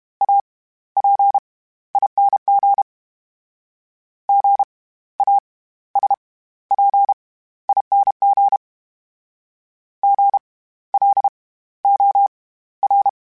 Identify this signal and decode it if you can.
Morse code, heard as a keyed tone, about 32 wpm, spaced as for 16 wpm, 800 Hz. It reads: APING GASPING GLOR
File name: Special INGs / Special INGs-mp3s / SlowToFaster ing @30